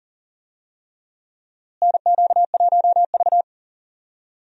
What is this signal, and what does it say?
Morse code, heard as a keyed tone, about 40 wpm, 700 Hz. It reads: NQ1V
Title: USA Call Signs - 4 Characters - Repeated 3x - 40wpm